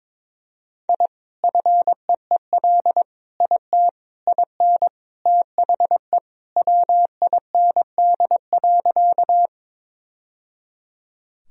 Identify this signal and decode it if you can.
Morse code, heard as a keyed tone, about 22 wpm, 700 Hz. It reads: I FEEL IT IN THE WIND.
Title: Sentences from Top 1000 Words - Review Part 2 - 22wpm